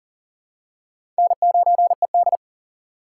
Morse code, heard as a keyed tone, about 40 wpm, 700 Hz. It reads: N9ED
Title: USA Call Signs - 4 Characters - 3x Character Spacing - 40wpm